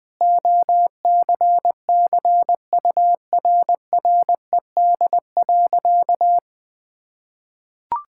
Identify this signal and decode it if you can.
Morse code, heard as a keyed tone, about 20 wpm, 700 Hz. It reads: OCCURRED.
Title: Sentences with 6-Word Phrases - 20wpm